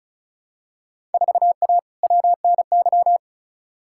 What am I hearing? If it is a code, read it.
Morse code, heard as a keyed tone, about 34 wpm, 700 Hz. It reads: 4A WNY
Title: Field Day Exchanges Encore - 35wpm